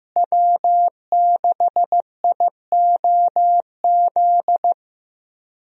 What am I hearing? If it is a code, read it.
Morse code, heard as a keyed tone, about 15 wpm, 700 Hz. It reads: W6IOZ
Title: USA Call Signs - 5 Characters - Repeated 2x - 15wpm